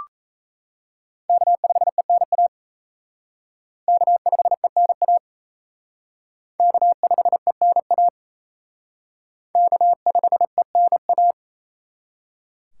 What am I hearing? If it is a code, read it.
Morse code, heard as a keyed tone, about 35 wpm, 700 Hz. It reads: K5ENA K5ENA K5ENA K5ENA
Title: Speed Racing - USA 1x3 Call Signs 28wpm